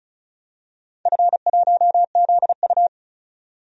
Morse code, heard as a keyed tone, about 36 wpm, 700 Hz. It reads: F1ZU
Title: DXCC-100 Call Signs - 4 Characters - 35wpm